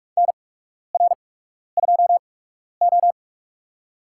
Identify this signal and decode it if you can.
Morse code, heard as a keyed tone, about 44 wpm, spaced as for 12 wpm, 700 Hz. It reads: NR2O
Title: USA Call Signs - 4 Characters - 8x Character Spacing - 45wpm